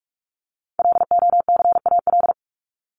Morse code, heard as a keyed tone, about 44 wpm, 700 Hz. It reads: LOCAL
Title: Top 501-750 Words 45wpm